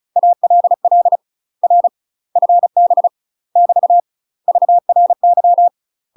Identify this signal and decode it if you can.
Morse code, heard as a keyed tone, about 36 wpm, 700 Hz. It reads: ALL R FB <BT> VRY
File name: UK MASTER QSOs TRAINING / 35WPM / UK MASTER QSOs 0010